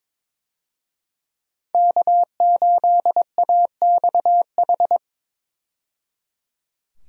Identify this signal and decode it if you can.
Morse code, heard as a keyed tone, about 22 wpm, 700 Hz. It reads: K8AXH
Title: USA Call Signs - 5 Characters - 22wpm